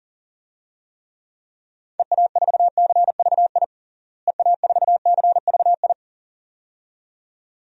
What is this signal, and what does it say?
Morse code, heard as a keyed tone, about 40 wpm, 700 Hz. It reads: EA4CVI EA4CVI
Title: DXCC-100 Call Signs - 6 Characters - Repeated 2x - 40wpm